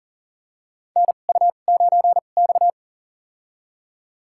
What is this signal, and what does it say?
Morse code, heard as a keyed tone, about 40 wpm, spaced as for 29 wpm, 700 Hz. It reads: NU9X
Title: USA Call Signs - 4 Characters - 2x Character Spacing - 40wpm